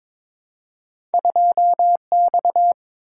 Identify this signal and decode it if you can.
Morse code, heard as a keyed tone, about 22 wpm, 700 Hz. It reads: 2X
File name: practice / 22wpm / Two Letter-Number Combinations - 22wpm